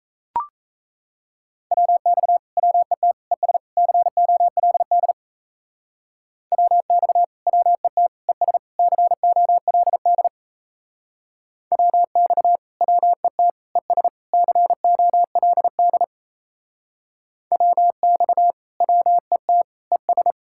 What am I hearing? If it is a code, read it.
Morse code, heard as a keyed tone, about 38 wpm, 700 Hz. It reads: WX WET ES COLD WX WET ES COLD WX WET ES COLD WX WET ES